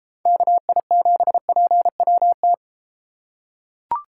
Morse code, heard as a keyed tone, about 34 wpm, 700 Hz. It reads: KI7PWT